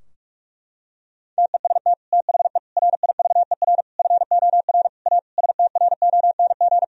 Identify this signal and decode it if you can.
Morse code, heard as a keyed tone, about 41 wpm, 700 Hz. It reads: TEST THE RIVER FOR A STRONG
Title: Sentences from Top 1000 Words - Review Part 4 - 45wpm